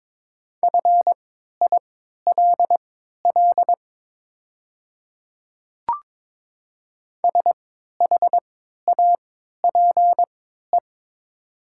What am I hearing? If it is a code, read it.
Morse code, heard as a keyed tone, about 22 wpm, spaced as for 12 wpm, 700 Hz. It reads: FILL SHAPE
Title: 3x Character Spacing - Top 500 Words - 22wpm